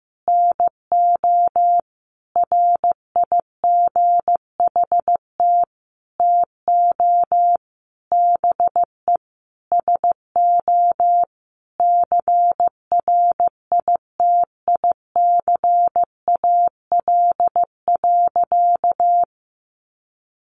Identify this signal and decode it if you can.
Morse code, heard as a keyed tone, about 15 wpm, 700 Hz. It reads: NO RIGHT TO BE SO CRITICAL.
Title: Sentences from Top 1100 Words 15wpm